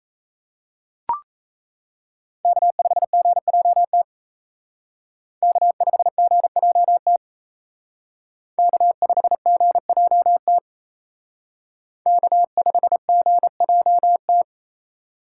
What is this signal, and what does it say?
Morse code, heard as a keyed tone, about 35 wpm, 700 Hz. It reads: K5GJT K5GJT K5GJT K5GJT